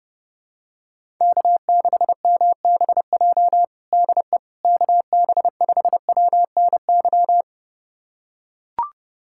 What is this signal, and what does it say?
Morse code, heard as a keyed tone, about 30 wpm, 700 Hz. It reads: K6MBJ DE KB5WNY
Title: USA Callsign DE Callsign 30wpm